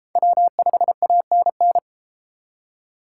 Morse code, heard as a keyed tone, about 31 wpm, 700 Hz. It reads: W5ANN